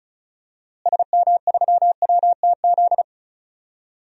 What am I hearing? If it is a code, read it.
Morse code, heard as a keyed tone, about 36 wpm, 700 Hz. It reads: SM3WTZ
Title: DXCC-100 Call Signs - 6 Characters - Repeated 2x - 35wpm